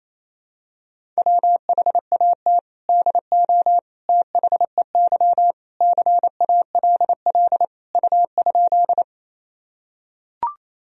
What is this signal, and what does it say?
Morse code, heard as a keyed tone, about 28 wpm, 700 Hz. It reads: WHAT DO THEY CALL U?
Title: Sentences from Top 100 Words and Abbreviations 28wpm